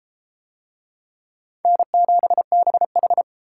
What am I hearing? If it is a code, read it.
Morse code, heard as a keyed tone, about 29 wpm, 700 Hz. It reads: N7BH